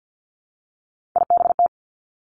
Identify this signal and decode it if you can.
Morse code, heard as a keyed tone, about 50 wpm, 700 Hz. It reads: IBT
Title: Three Challenging Letter-Number Combinations 50wpm